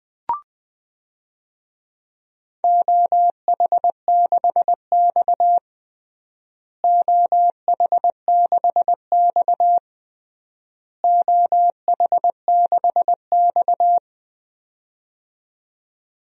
Morse code, heard as a keyed tone, about 21 wpm, 700 Hz. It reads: OH6X OH6X OH6X
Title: DXCC-100 Call Signs - 4 Characters - Repeated 3x - 20wpm